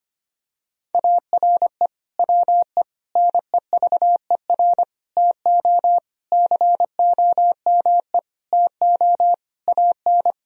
Morse code, heard as a keyed tone, about 25 wpm, 700 Hz. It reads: ARE WE NEVER TO COME TO AN
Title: Sentences from Top 1000 Words - Review Part 3 - 25wpm